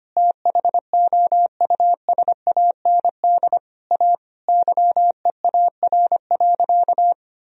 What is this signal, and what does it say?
Morse code, heard as a keyed tone, about 25 wpm, 700 Hz. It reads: THOUSAND A YEAR.
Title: Sentences from Top 1000 Words - Review Part 3 - 25wpm